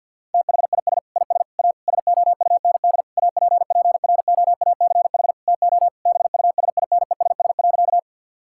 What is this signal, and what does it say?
Morse code, heard as a keyed tone, about 50 wpm, 700 Hz. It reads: THIS IS A SOUND APPROACH TO BUSINESS.
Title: Sentences from Top 1000 Words - Review Part 6 - 50wpm